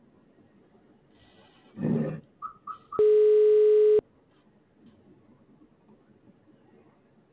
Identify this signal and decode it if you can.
Ringback tone (Europe, 425 Hz).